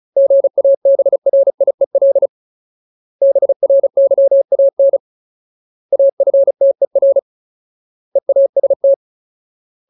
Morse code, heard as a keyed tone, about 34 wpm, 550 Hz. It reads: GABRIEL BRYAN AFTER EAST
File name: Ultimate Word List / 35WPM Ultimate